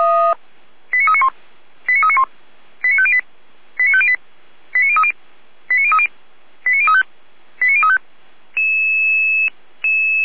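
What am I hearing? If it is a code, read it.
Five-tone selective call (ZVEI), heard as quick runs of five tones, tones of about 70 ms. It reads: 89391 89391 89508 89508 89030 89030 89035 89035
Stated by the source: ZVEI five-tone call received on SDR.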